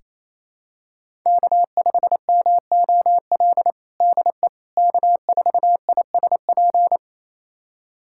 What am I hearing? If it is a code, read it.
Morse code, heard as a keyed tone, about 28 wpm, 700 Hz. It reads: K5MOL DE K4ISP
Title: USA Callsign DE Callsign - 5 Characters - 28wpm